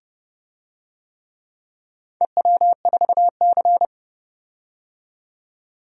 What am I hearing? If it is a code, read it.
Morse code, heard as a keyed tone, about 30 wpm, 700 Hz. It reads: EW4C
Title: DXCC-100 Call Signs - 4 Characters - 30wpm